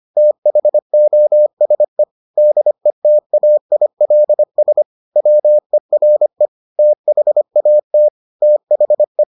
Morse code, heard as a keyed tone, about 25 wpm, 600 Hz. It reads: THOSE DETAILS WERE THAT THE